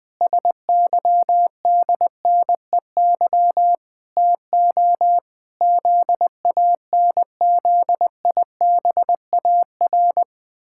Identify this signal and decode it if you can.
Morse code, heard as a keyed tone, about 20 wpm, 700 Hz. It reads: SYDNEY TO ZANZIBAR
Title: Pangram Challenge v2 - 20wpm